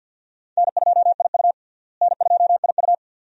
Morse code, heard as a keyed tone, about 52 wpm, 700 Hz. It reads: N2IU N2IU
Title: USA Call Signs - 4 Characters - Repeated 2x - 50wpm